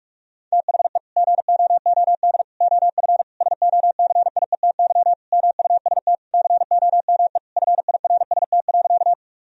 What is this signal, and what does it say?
Morse code, heard as a keyed tone, about 46 wpm, 700 Hz. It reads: THE GOOD OF SOCIETY MUST COME FIRST.
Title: Sentences from Top 1000 Words - Review Part 4 - 45wpm